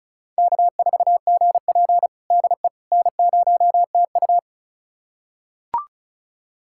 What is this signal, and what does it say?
Morse code, heard as a keyed tone, about 34 wpm, 700 Hz. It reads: K4GP DE N0TU
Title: USA Callsign DE Callsign - 4 Characters - 35wpm